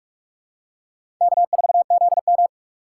Morse code, heard as a keyed tone, about 46 wpm, 700 Hz. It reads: K4ZM
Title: USA Call Signs - 4 Characters - 6x Character Spacing - 45wpm